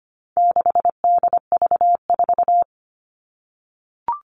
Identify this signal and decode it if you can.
Morse code, heard as a keyed tone, about 25 wpm, 700 Hz. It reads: 6DV4
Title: All Cut Numbers - 4-Digit 25wpm